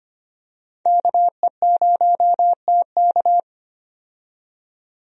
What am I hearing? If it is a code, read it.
Morse code, heard as a keyed tone, about 25 wpm, 700 Hz. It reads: KE0TK
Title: USA Call Signs - 5 Characters - 25wpm